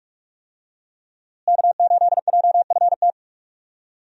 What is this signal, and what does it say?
Morse code, heard as a keyed tone, about 42 wpm, 700 Hz. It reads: K8JFT
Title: USA Call Signs - 5 Characters - Repeated 2x - 45wpm